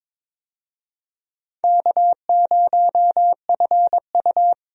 Morse code, heard as a keyed tone, about 23 wpm, 700 Hz. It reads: K0FU
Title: USA Call Signs - 4 Characters - 22wpm